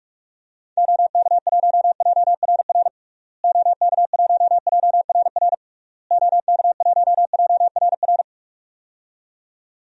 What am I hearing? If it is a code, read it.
Morse code, heard as a keyed tone, about 46 wpm, 700 Hz. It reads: OK1JRR OK1JRR OK1JRR
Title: DXCC-100 Call Signs - 6 Characters - Repeated 3x - 45wpm